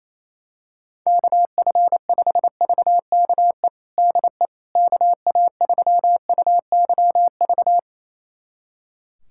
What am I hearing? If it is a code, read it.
Morse code, heard as a keyed tone, about 28 wpm, 700 Hz. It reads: KF5VKE DE KA3UYV
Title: USA Callsign DE Callsign - 6 Characters - 28wpm